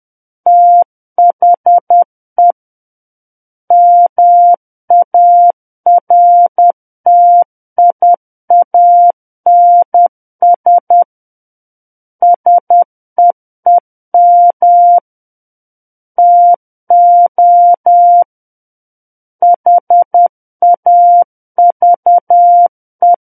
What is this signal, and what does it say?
Morse code, heard as a keyed tone, about 10 wpm, 700 Hz. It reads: THE MARTIANS SEEM TO HAVE
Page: War of the Worlds - 01-Chapter 1 - 10 WPM